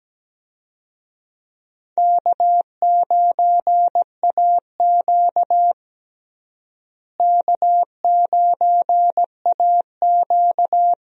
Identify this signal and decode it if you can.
Morse code, heard as a keyed tone, about 17 wpm, 700 Hz. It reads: K9AQ K9AQ
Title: USA Call Signs - 4 Characters - Repeated 2x - 17wpm